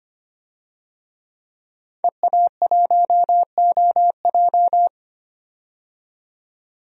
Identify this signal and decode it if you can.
Morse code, heard as a keyed tone, about 25 wpm, 700 Hz. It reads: EA1OJ
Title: DXCC-100 Call Signs - 5 Characters - 25wpm